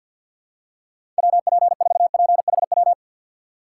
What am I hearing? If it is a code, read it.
Morse code, heard as a keyed tone, about 52 wpm, 700 Hz. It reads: WP4PHW
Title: USA Call Signs - 6 Characters - 3x Character Spacing - 50wpm